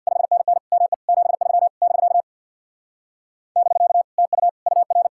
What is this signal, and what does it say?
Morse code, heard as a keyed tone, about 64 wpm, 700 Hz. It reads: 5NN ME 73 <BK> <BK> TU UR